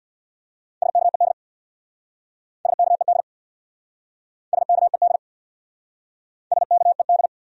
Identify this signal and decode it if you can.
Morse code, heard as a keyed tone, about 69 wpm, 700 Hz. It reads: SKED SKED SKED SKED